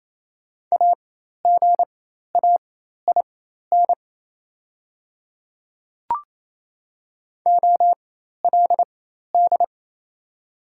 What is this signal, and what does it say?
Morse code, heard as a keyed tone, about 29 wpm, spaced as for 13 wpm, 690 Hz. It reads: AGAIN OLD
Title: 4x Character Spacing - Top 200 Words - 28wpm